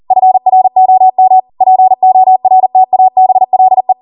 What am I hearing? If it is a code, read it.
Morse code, heard as a keyed tone, about 40 wpm, 750 Hz. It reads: FROM PORTABLE